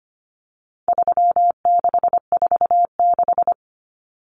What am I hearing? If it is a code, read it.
Morse code, heard as a keyed tone, about 25 wpm, 700 Hz. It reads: 3646
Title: Common Cut Numbers - 4-Digit 25wpm